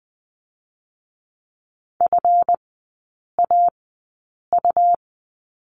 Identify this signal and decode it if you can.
Morse code, heard as a keyed tone, about 20 wpm, spaced as for 8 wpm, 700 Hz. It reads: FAU
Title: 138 Lesson 23e - Letter f - Three Characters 20wpm